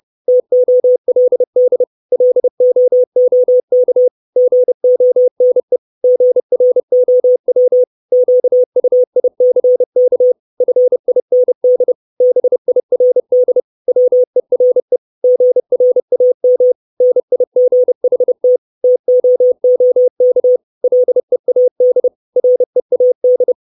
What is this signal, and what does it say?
Morse code, heard as a keyed tone, about 30 wpm, 500 Hz. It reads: TOLD LOOK GONE GROW QUICK FIND BIRD WERE GRAM NIGHT TOOK LEAD READ
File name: 632 words / 632 Words 30WPM